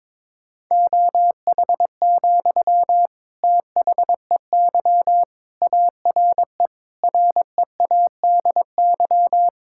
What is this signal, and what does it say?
Morse code, heard as a keyed tone, about 22 wpm, 700 Hz. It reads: OH, THEY ARE READY
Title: Sentences from Top 1600 Words 22wpm